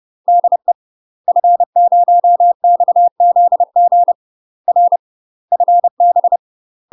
Morse code, heard as a keyed tone, about 30 wpm, 700 Hz. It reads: DE F0XZG R FB
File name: UK MASTER QSOs TRAINING / 30WPM / UK MASTER QSOs 0040